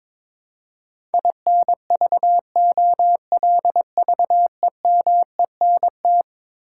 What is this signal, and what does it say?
Morse code, heard as a keyed tone, about 22 wpm, 700 Hz. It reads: INVOLVEMENT
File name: practice / 22wpm / Eleven Letter Words - 22wpm